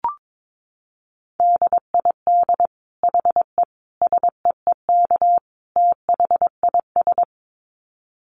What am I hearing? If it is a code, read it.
Morse code, heard as a keyed tone, about 22 wpm, 700 Hz. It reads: DID HE SEEK THIS
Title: Dit Challenge 22wpm